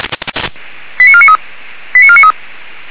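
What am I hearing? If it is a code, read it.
Five-tone selective call (ZVEI), heard as quick runs of five tones, tones of about 70 ms. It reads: 89493 89493
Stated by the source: ZVEI five-tone call received on SDR.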